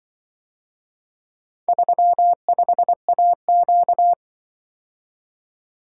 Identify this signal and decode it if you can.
Morse code, heard as a keyed tone, about 24 wpm, 700 Hz. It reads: 35AQ
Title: Speed-Racing - Four Letter-Number Combinations - 20wpm